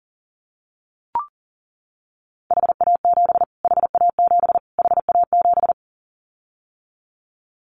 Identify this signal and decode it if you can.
Morse code, heard as a keyed tone, about 40 wpm, 700 Hz. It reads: HA7 HA7 HA7
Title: Callsign Prefixes for Top 25 DXCC - Repeated 3x - 40wpm